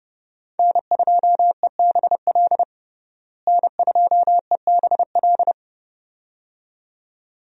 Morse code, heard as a keyed tone, about 30 wpm, 700 Hz. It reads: N2EBL N2EBL
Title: USA Call Signs - 5 Characters - Repeated 2x - 30wpm